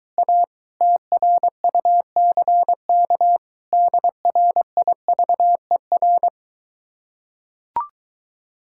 Morse code, heard as a keyed tone, about 23 wpm, 700 Hz. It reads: A TRUCK DRIVER